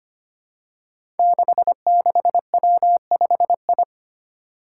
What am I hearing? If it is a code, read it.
Morse code, heard as a keyed tone, about 25 wpm, 700 Hz. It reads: 66W5I
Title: Five Challenging Letter-Number Combinations 25wpm